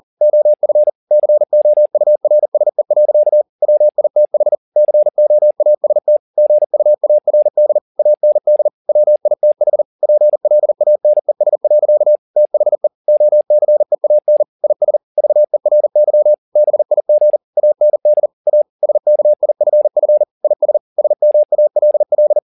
Morse code, heard as a keyed tone, about 40 wpm, 600 Hz. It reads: OF COURSE. WITH COAST GUARD AND WITH PLANES. THE OCEAN IS VERY BIG AND A SKIFF IS SMALL
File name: the old man and the sea 2 40wpm